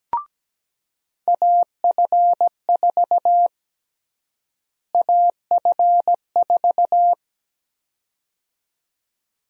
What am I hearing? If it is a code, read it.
Morse code, heard as a keyed tone, about 17 wpm, 700 Hz. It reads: AF4 AF4
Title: USA Call Sign Prefixes - Repeated 2x - 17wpm